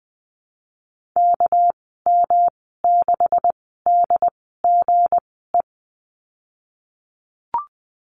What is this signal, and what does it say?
Morse code, heard as a keyed tone, about 20 wpm, spaced as for 15 wpm, 700 Hz. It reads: KM6DGE
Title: 200 Lesson 32g - Letter k - Call Signs 20wpm